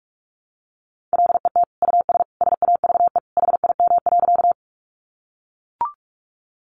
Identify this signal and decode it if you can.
Morse code, heard as a keyed tone, about 44 wpm, 700 Hz. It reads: LET US SAVE HIM.
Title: Sentences from Top 1100 Words 45wpm